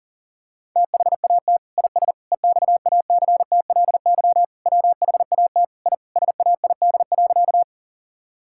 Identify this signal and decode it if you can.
Morse code, heard as a keyed tone, about 40 wpm, 700 Hz. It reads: THAT IS EXACTLY WHAT I SAID.
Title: Sentences from Top 1000 Words - Review Part 6 - 40wpm